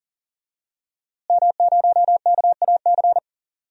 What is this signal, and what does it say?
Morse code, heard as a keyed tone, about 40 wpm, 700 Hz. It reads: M0KAC
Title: DXCC-100 Call Signs - 5 Characters - Repeated 3x - 40wpm